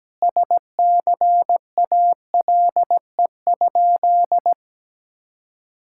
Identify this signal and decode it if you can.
Morse code, heard as a keyed tone, about 17 wpm, 700 Hz. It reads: SCALE?
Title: Sentences from Top 1000 Words - Review Part 5 - 17wpm